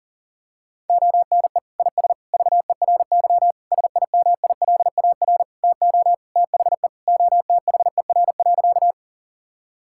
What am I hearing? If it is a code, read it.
Morse code, heard as a keyed tone, about 37 wpm, 700 Hz. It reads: ONE IS VERY SIMILAR TO THE OTHER.
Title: Sentences from Top 1000 Words - Review Part 5 - 40wpm